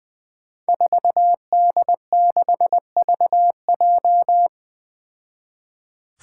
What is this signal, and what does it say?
Morse code, heard as a keyed tone, about 21 wpm, 700 Hz. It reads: 4D6VJ
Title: Five Challenging Letter-Number Combinations 20wpm